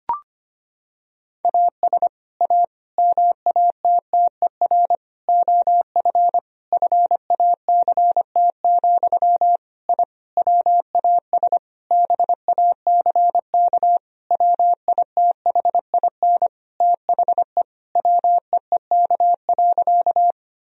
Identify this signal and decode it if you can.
Morse code, heard as a keyed tone, about 25 wpm, 700 Hz. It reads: AS A MATTER OF FACT, I WAS BACK WITHIN THE WEEK.
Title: Sentences from Top 1000 Words - Review Part 4 - 25wpm